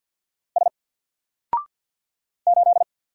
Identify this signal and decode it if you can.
Morse code, heard as a keyed tone, about 50 wpm, 700 Hz. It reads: S 8
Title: Single Character (Letter:Number:Punctuation:Prosign) 50wpm